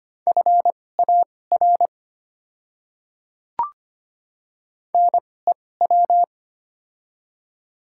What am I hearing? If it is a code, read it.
Morse code, heard as a keyed tone, about 25 wpm, spaced as for 18 wpm, 700 Hz. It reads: FAR NEW
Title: 2x Character Spacing - Top 300 Words - 25wpm